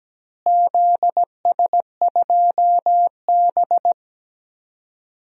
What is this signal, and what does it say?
Morse code, heard as a keyed tone, about 17 wpm, 700 Hz. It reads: ZS2B
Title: DXCC-100 Call Signs - 4 Characters - Repeated 2x - 17wpm